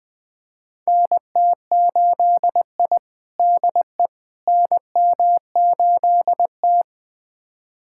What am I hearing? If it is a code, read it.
Morse code, heard as a keyed tone, about 20 wpm, 700 Hz. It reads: NT8I DE NM8T
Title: USA Callsign DE Callsign - 4 Characters - 20wpm